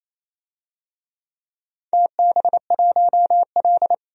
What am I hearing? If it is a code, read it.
Morse code, heard as a keyed tone, about 28 wpm, 700 Hz. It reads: TB1L
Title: DXCC-100 Call Signs - 4 Characters - 28wpm